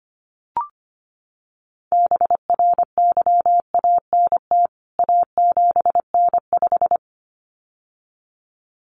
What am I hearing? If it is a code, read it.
Morse code, heard as a keyed tone, about 25 wpm, 700 Hz. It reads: BRYANT A7N5
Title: CWT Exchange 25wpm